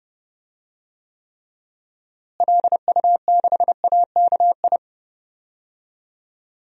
Morse code, heard as a keyed tone, about 32 wpm, 700 Hz. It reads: LU6AKI
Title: DXCC-100 Call Signs - 6 Characters - Repeated 3x - 30wpm